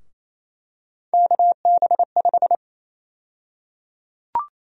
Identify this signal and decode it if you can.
Morse code, heard as a keyed tone, about 27 wpm, 700 Hz. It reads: KB5
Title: USA Call Sign Prefixes - Repeated 2x - 28wpm